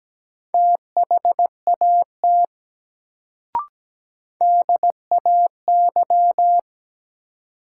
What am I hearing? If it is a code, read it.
Morse code, heard as a keyed tone, about 17 wpm, 700 Hz. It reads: THAT DAY
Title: Sets of 2 Words - New Word Drill - 17wpm